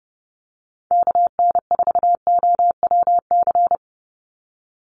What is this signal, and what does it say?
Morse code, heard as a keyed tone, about 30 wpm, 700 Hz. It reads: KN4OWC
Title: USA Call Signs 30wpm